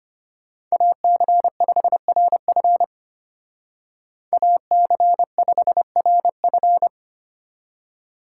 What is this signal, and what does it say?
Morse code, heard as a keyed tone, about 32 wpm, 700 Hz. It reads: AC5RF AC5RF